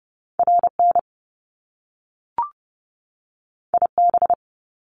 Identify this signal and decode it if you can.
Morse code, heard as a keyed tone, about 30 wpm, 700 Hz. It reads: RN IB
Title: Top Two Letter Combinations in Words 30wpm